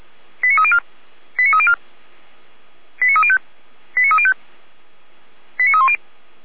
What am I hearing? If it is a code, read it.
Five-tone selective call (ZVEI), heard as quick runs of five tones, tones of about 70 ms. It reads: 89304 89304 89305 89305 89310